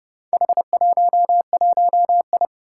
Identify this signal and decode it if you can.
Morse code, heard as a keyed tone, about 30 wpm, 700 Hz. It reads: H11I
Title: Four Challenging Letter-Number Combinations 30wpm